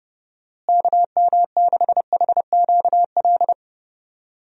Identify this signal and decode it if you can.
Morse code, heard as a keyed tone, about 30 wpm, 700 Hz. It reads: KM6HQL
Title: USA Call Signs - 6 Characters - Repeated 3x - 30wpm